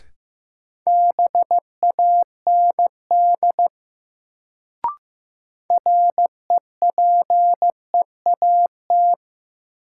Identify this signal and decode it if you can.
Morse code, heard as a keyed tone, about 15 wpm, 700 Hz. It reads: BAND REPEAT
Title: Sentences from Top 1600 Words - New Word Drill - 15wpm